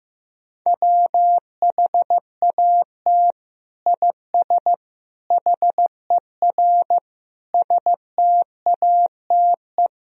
Morse code, heard as a keyed tone, about 15 wpm, 700 Hz. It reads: WHAT IS HER STATE
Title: Sentences from Top 1000 Words - Review Part 2 - 15wpm